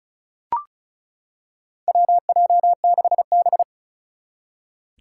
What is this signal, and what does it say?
Morse code, heard as a keyed tone, about 36 wpm, 700 Hz. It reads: WJ6B
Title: Four Challenging Letter-Number Combinations 35wpm